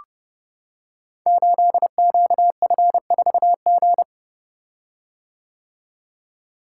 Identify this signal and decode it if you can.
Morse code, heard as a keyed tone, about 29 wpm, 700 Hz. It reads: 8QF4G
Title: Code Groups - 5 Letter:Numbers - 30wpm